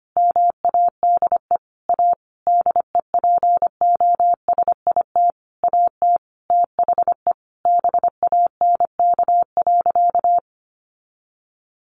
Morse code, heard as a keyed tone, about 25 wpm, 700 Hz. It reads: MADE A DEPOSIT AT THE BANK.
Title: Sentences from Top 1400 Words 25wpm